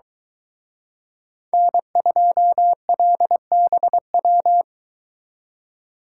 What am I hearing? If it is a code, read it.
Morse code, heard as a keyed tone, about 22 wpm, 700 Hz. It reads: N2LBW